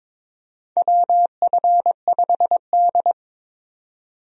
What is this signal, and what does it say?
Morse code, heard as a keyed tone, about 22 wpm, 700 Hz. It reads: WF5D
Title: USA Call Signs - 4 Characters - 22wpm